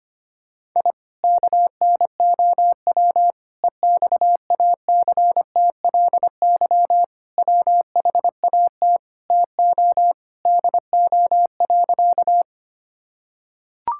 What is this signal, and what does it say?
Morse code, heard as a keyed tone, about 25 wpm, 700 Hz. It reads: I KNOW EXACTLY WHAT TO DO.
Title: Sentences from Top 1000 Words - Review Part 6 - 25wpm